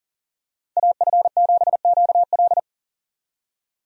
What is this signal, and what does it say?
Morse code, heard as a keyed tone, about 40 wpm, 700 Hz. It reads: AF7QL
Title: USA Call Signs - 5 Characters - 40wpm